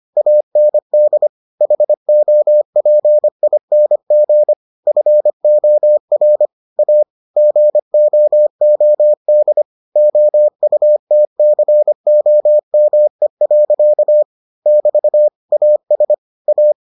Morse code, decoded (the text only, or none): AND HOPING FOR A GOOD OUTCOME. <BT> AS A